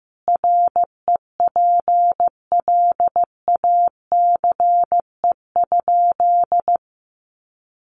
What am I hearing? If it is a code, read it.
Morse code, heard as a keyed tone, about 15 wpm, 700 Hz. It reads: REPLACE?
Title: Sentences from Top 1400 Words 15wpm